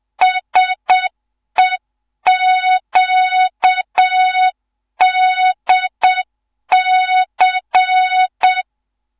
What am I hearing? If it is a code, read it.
Morse code, heard as a keyed tone, about 7 wpm, 3700 Hz. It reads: SEQDC
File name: Morze-HA2MN / hang / 15 GYAKORLAT